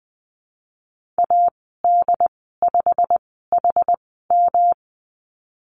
Morse code, heard as a keyed tone, about 20 wpm, spaced as for 15 wpm, 700 Hz. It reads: AD5HM
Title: 147 Lesson 24g - Letter y - Call Signs 20wpm